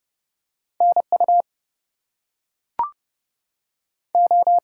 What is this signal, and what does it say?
Morse code, heard as a keyed tone, about 30 wpm, 700 Hz. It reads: NU O
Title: US States and Canadian Provinces 30wpm